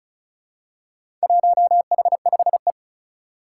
Morse code, heard as a keyed tone, about 36 wpm, 700 Hz. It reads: 1H5E
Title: Four Challenging Letter-Number Combinations 35wpm